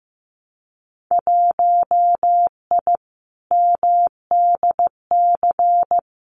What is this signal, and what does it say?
Morse code, heard as a keyed tone, about 15 wpm, 700 Hz. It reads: 1I MDC
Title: Winter Field Day Exchanges 15wpm